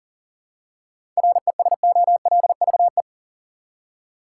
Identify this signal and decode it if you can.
Morse code, heard as a keyed tone, about 40 wpm, 700 Hz. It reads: RESOLVE
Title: Seven Letter Words - 40wpm